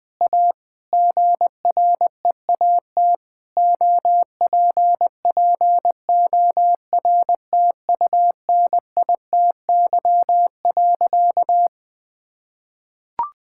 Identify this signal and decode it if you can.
Morse code, heard as a keyed tone, about 20 wpm, 700 Hz. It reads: A GREAT OPPORTUNITY.
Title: Sentences with 4-Word Phrases - 20wpm